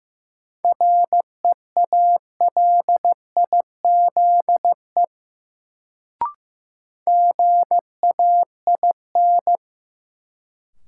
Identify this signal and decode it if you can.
Morse code, heard as a keyed tone, about 15 wpm, 700 Hz. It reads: REALIZE GAIN
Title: Sentences from Top 1000 Words - New Word Drill - 15wpm